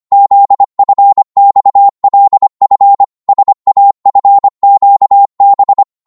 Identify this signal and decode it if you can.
Morse code, heard as a keyed tone, about 25 wpm, 800 Hz. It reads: ZFXLF SAFQB
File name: Morse / ltr-25wpm